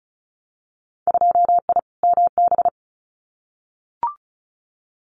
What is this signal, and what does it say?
Morse code, heard as a keyed tone, about 36 wpm, 700 Hz. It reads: 2I MB
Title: Winter Field Day Exchanges Encore 35wpm